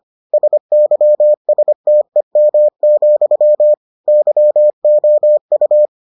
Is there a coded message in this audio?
Yes: Morse code from 0.3 to 5.9 s, heard as a keyed tone, about 25 wpm, 600 Hz. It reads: SYSTEM, YOU